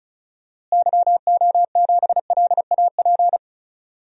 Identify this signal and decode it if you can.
Morse code, heard as a keyed tone, about 36 wpm, 700 Hz. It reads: YO7LAP
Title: DXCC-100 Call Signs - 6 Characters - Repeated 3x - 35wpm